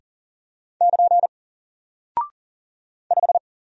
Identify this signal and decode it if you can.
Morse code, heard as a keyed tone, about 40 wpm, 700 Hz. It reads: <KN> 5
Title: Single Character (Letter:Number:Punctuation:Prosign) 40wpm